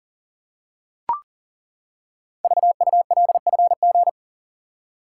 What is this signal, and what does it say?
Morse code, heard as a keyed tone, about 39 wpm, 700 Hz. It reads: VULFG
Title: Code Groups - 5 Letters - 40wpm